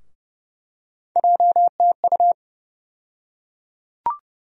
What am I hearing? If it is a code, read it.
Morse code, heard as a keyed tone, about 31 wpm, 700 Hz. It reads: JTU
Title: USA Call Sign Postfixes - 3 Letters 30wpm